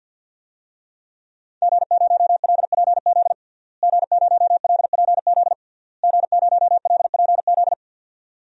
Morse code, heard as a keyed tone, about 51 wpm, 700 Hz. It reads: G0LPZ G0LPZ G0LPZ
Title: DXCC-100 Call Signs - 5 Characters - Repeated 3x - 50wpm